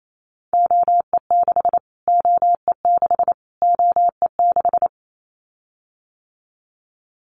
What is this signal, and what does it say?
Morse code, heard as a keyed tone, about 28 wpm, 700 Hz. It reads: OE6 OE6 OE6
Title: Callsign Prefixes for Top 25 DXCC - Repeated 3x - 28wpm